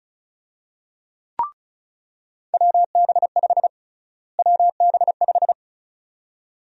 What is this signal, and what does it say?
Morse code, heard as a keyed tone, about 36 wpm, 700 Hz. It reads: WB5 WB5
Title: USA Call Sign Prefixes - Repeated 2x - 35wpm